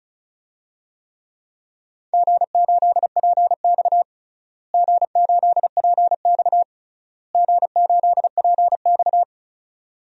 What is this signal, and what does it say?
Morse code, heard as a keyed tone, about 34 wpm, 700 Hz. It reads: G8PX G8PX G8PX
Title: DXCC-100 Call Signs - 4 Characters - Repeated 3x - 35wpm